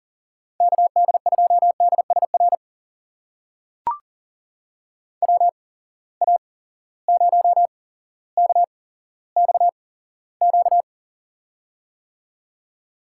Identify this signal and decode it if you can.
Morse code, heard as a keyed tone, about 40 wpm, spaced as for 17 wpm, 700 Hz. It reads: KD2DSR WA0KXQ
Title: USA Call Signs - 6 Characters - 8x Character Spacing - 40wpm